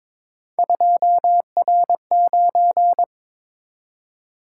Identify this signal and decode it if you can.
Morse code, heard as a keyed tone, about 21 wpm, 700 Hz. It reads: 2R9
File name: practice / 22wpm / Three Letter-Number Combinations - 22wpm